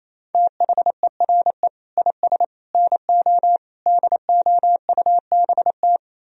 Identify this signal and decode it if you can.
Morse code, heard as a keyed tone, about 35 wpm, 700 Hz. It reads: THERE IS NO DOUBT
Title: Speed-Racing - Sets of 5 Words 28wpm